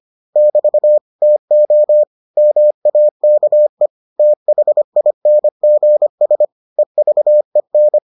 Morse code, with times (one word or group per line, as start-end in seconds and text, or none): 0.4-1.0 <BT>
1.2-2.0 TO
2.4-3.9 MAKE
4.2-6.5 THINGS
6.8-8.0 EVEN